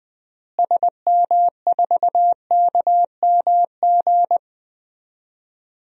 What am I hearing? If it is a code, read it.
Morse code, heard as a keyed tone, about 21 wpm, 700 Hz. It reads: SM4KMG
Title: DXCC-100 Call Signs - 6 Characters - Repeated 3x - 20wpm